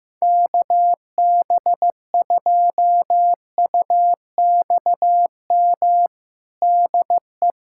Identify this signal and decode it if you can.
Morse code, heard as a keyed tone, about 15 wpm, 700 Hz. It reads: KB2UXM DE
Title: USA Callsign DE Callsign - 6 Characters - 15wpm